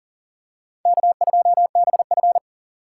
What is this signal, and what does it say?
Morse code, heard as a keyed tone, about 40 wpm, 700 Hz. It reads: K2BF
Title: USA Call Signs - 4 Characters - 4x Character Spacing - 40wpm